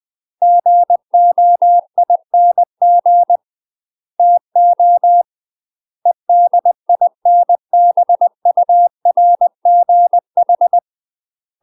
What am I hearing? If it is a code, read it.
Morse code, heard as a keyed tone, about 20 wpm, 700 Hz. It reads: GOING TO EDINBURGH
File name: UK MASTER QSOs TRAINING / 20WPM / UK MASTER QSOs 0014